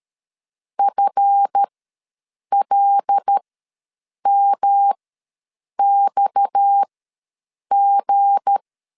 Morse code, heard as a keyed tone, about 13 wpm, spaced as for 7 wpm, 800 Hz. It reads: FLMXG
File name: LEZIONI / 7 GF9Z / 7 GF ws 12wpm 3140